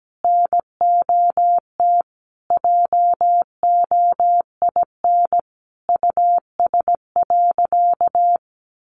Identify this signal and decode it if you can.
Morse code, heard as a keyed tone, about 17 wpm, 700 Hz. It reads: NOT JOIN US.
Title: Sentences from Top 1400 Words 17wpm